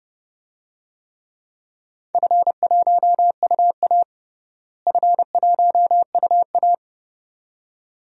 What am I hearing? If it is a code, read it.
Morse code, heard as a keyed tone, about 32 wpm, 700 Hz. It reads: F1UA F1UA
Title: DXCC-100 Call Signs - 4 Characters - Repeated 2x - 30wpm